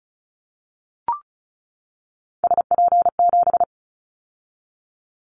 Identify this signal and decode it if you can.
Morse code, heard as a keyed tone, about 34 wpm, 700 Hz. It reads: SP7
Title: Callsign Prefixes for Top 25 DXCC - 35wpm